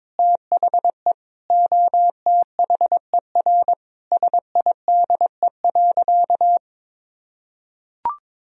Morse code, heard as a keyed tone, about 22 wpm, 700 Hz. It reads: THE OTHER SIDE.
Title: Sentences from Top 1000 Words - Review Part 1 - 22wpm